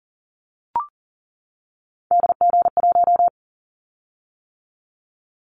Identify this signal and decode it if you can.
Morse code, heard as a keyed tone, about 40 wpm, 700 Hz. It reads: DG1
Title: Callsign Prefixes for Top 25 DXCC - 40wpm